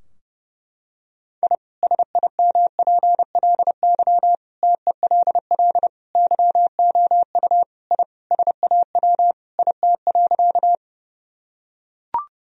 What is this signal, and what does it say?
Morse code, heard as a keyed tone, about 30 wpm, 700 Hz. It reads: I SIMPLY TELL YOU I SAW IT.
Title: Sentences from Top 1000 Words - Review Part 4 - 30wpm